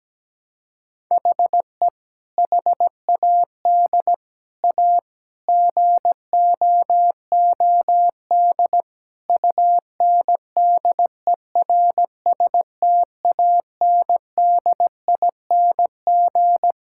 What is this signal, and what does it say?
Morse code, heard as a keyed tone, about 17 wpm, 700 Hz. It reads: HE HAD A GOOD UNDERSTANDING
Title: Sentences from Top 1000 Words - Review Part 6 - 17wpm